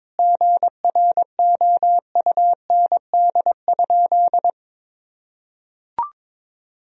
Morse code, heard as a keyed tone, about 22 wpm, 700 Hz. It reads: GROUND?
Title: Sentences from Top 1000 Words - Review Part 2 - 22wpm